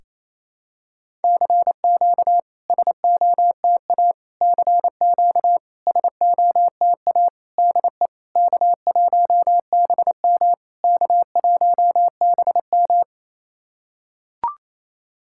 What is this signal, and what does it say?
Morse code, heard as a keyed tone, about 28 wpm, 700 Hz. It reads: CQ SOTA CQ SOTA DE K1BM K1BM
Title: SOTA QSO Practice 28wpm